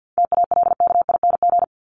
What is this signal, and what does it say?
Morse code, heard as a keyed tone, about 50 wpm, 700 Hz. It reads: TALKING
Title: ING Suffix - Top 100 Words with Base Words 50wpm